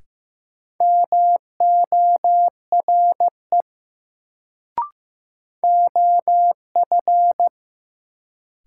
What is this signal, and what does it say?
Morse code, heard as a keyed tone, about 15 wpm, 700 Hz. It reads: MORE OF
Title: Sets of 2 Words - New Word Drill - 15wpm